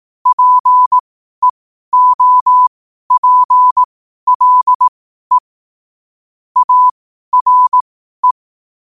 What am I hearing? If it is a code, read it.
Morse code, heard as a keyed tone, about 18 wpm, spaced as for 12 wpm, 1000 Hz. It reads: PEOPLE ARE
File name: qotdfiles / 12wpm